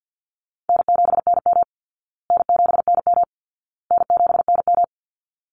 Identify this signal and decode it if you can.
Morse code, heard as a keyed tone, about 48 wpm, 700 Hz. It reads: N7NM N7NM N7NM
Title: Easy Call Signs Repeated 3x - 50wpm